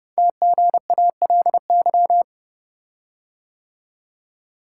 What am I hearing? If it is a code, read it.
Morse code, heard as a keyed tone, about 29 wpm, 700 Hz. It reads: TGALY
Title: Code Groups - 5 Letters - 30wpm